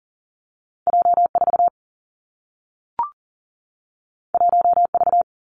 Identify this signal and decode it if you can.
Morse code, heard as a keyed tone, about 40 wpm, 700 Hz. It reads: J 4 1 V
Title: Two Challenging Letter-Number Combinations 40wpm